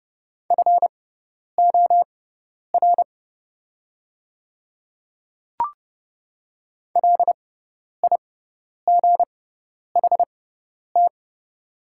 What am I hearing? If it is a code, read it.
Morse code, heard as a keyed tone, about 30 wpm, spaced as for 10 wpm, 700 Hz. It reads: FOR LIGHT
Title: 6x Character Spacing - Top 300 Words - 30wpm